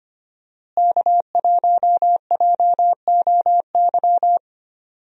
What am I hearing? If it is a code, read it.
Morse code, heard as a keyed tone, about 25 wpm, 700 Hz. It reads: K1JOY
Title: USA Call Signs - 5 Characters - Repeated 2x - 25wpm